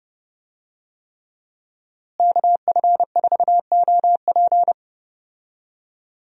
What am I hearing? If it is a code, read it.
Morse code, heard as a keyed tone, about 30 wpm, 700 Hz. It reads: KF4OP
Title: USA Call Signs - 5 Characters - Repeated 3x - 30wpm